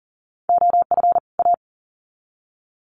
Morse code, heard as a keyed tone, about 40 wpm, 700 Hz. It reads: OF A
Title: Sets of 2 Words 40wpm